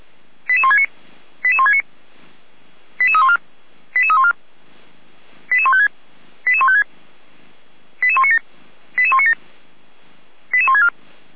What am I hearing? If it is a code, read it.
Five-tone selective call (ZVEI), heard as quick runs of five tones, tones of about 70 ms. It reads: 88169 88169 88314 88314 88156 88156 88197 88197 88164
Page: /455/SDR/ZVEI/